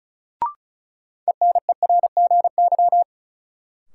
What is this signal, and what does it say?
Morse code, heard as a keyed tone, about 36 wpm, 700 Hz. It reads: ENERGY
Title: Sentences from Top 900 Words - New Word Drill - 35wpm